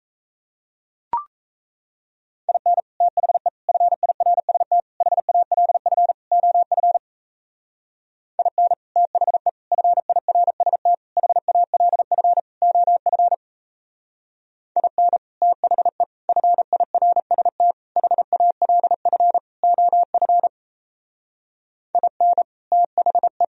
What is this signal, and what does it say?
Morse code, heard as a keyed tone, about 35 wpm, 700 Hz. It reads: IN THE FIRST HALF OF IN THE FIRST HALF OF IN THE FIRST HALF OF IN THE